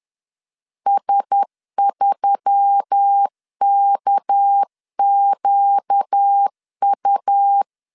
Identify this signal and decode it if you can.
Morse code, heard as a keyed tone, about 10 wpm, 800 Hz. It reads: S3KQU
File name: LEZIONI / 6 PKX0 / 6 PKX 10wpm 2426